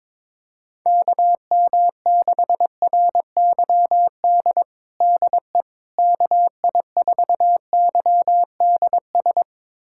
Morse code, heard as a keyed tone, about 22 wpm, 700 Hz. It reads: KM6RYD DE KI4YDS
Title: USA Callsign DE Callsign - 6 Characters - 22wpm